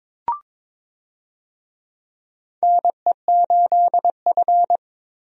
Morse code, heard as a keyed tone, about 23 wpm, 700 Hz. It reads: NE8F
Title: USA Call Signs - 4 Characters - 22wpm